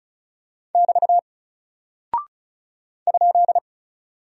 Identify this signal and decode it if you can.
Morse code, heard as a keyed tone, about 36 wpm, 700 Hz. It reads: <BT> ?
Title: Single Character (Letter:Number:Punctuation:Prosign) Alt 35wpm